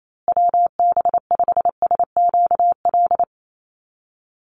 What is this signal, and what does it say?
Morse code, heard as a keyed tone, about 28 wpm, 700 Hz. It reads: WB5SQL
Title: USA Call Signs Repeated 3x 28wpm